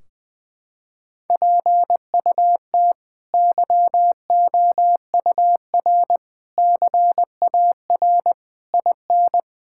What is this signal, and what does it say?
Morse code, heard as a keyed tone, about 20 wpm, 700 Hz. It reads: PUT YOUR CAR IN
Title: Sentences from Top 1600 Words 20wpm